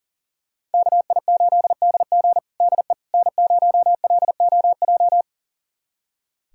Morse code, heard as a keyed tone, about 40 wpm, 700 Hz. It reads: KI8DG DE N0LOJ
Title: USA Callsign DE Callsign - 5 Characters - 40wpm